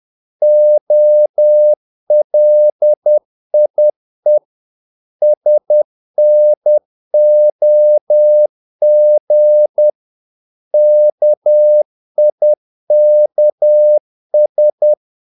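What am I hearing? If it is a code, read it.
Morse code, heard as a keyed tone, about 10 wpm, 600 Hz. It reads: OLIE SNOG KIKS
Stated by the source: Morse practice of short Danish words at 10 wpm, from CW med Gnister.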